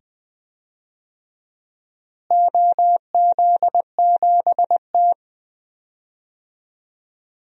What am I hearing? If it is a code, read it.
Morse code, heard as a keyed tone, about 20 wpm, 700 Hz. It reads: OZ7T
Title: DXCC-100 Call Signs - 4 Characters - Repeated 2x - 20wpm